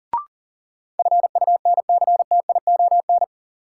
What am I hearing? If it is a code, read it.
Morse code, heard as a keyed tone, about 39 wpm, 700 Hz. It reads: FUNCTION
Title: Sentences from Top 1100 Words - New Word Drill Alt - 40wpm